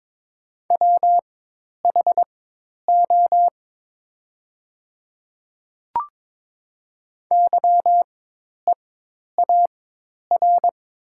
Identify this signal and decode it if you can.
Morse code, heard as a keyed tone, about 22 wpm, spaced as for 10 wpm, 700 Hz. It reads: WHO YEAR
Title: 4x Character Spacing - Top 200 Words - 22wpm